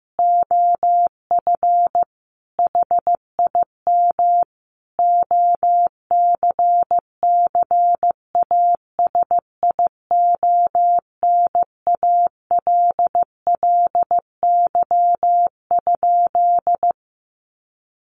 Morse code, heard as a keyed tone, about 15 wpm, 700 Hz. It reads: OF HIM OCCASIONALLY?
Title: Sentences from Top 1300 Words 15wpm